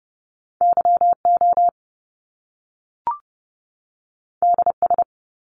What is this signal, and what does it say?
Morse code, heard as a keyed tone, about 30 wpm, 700 Hz. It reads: YO DS
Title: Top Two Letter Combinations in Words 30wpm